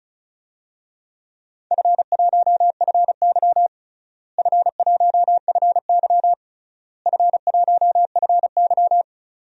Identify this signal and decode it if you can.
Morse code, heard as a keyed tone, about 34 wpm, 700 Hz. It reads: F1FY F1FY F1FY
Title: DXCC-100 Call Signs - 4 Characters - Repeated 3x - 35wpm